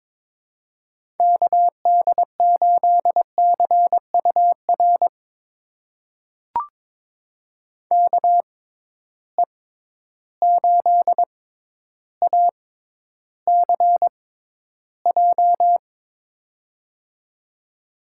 Morse code, heard as a keyed tone, about 22 wpm, spaced as for 11 wpm, 700 Hz. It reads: KD8CUR KE8ACJ
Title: USA Call Signs - 6 Characters - 6x Character Spacing - 22wpm